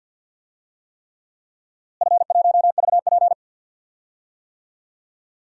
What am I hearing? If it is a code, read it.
Morse code, heard as a keyed tone, about 50 wpm, 700 Hz. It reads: F1VP
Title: DXCC-100 Call Signs - 4 Characters - Repeated 3x - 50wpm